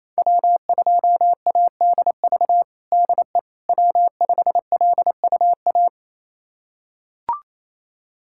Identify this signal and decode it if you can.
Morse code, heard as a keyed tone, about 28 wpm, 700 Hz. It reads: W2ADV DE W5LUA
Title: USA Callsign DE Callsign - 5 Characters - 28wpm